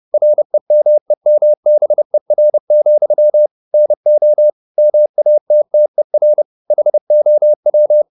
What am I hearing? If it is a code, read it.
Morse code, heard as a keyed tone, about 30 wpm, 600 Hz. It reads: REMEMBER, NO MATTER HOW
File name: the old man and the sea / the old man and the sea 1 30wpm